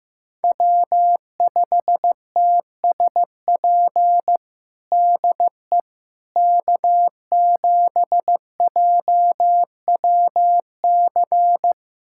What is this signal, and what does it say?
Morse code, heard as a keyed tone, about 15 wpm, 700 Hz. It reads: W5TSP DE K7JWC
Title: USA Callsign DE Callsign - 5 Characters - 15wpm